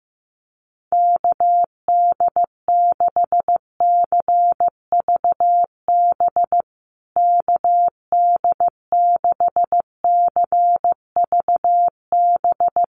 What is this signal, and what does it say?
Morse code, heard as a keyed tone, about 15 wpm, 700 Hz. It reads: KD6CVB KD6CVB
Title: USA Call Signs Repeated 3x 15wpm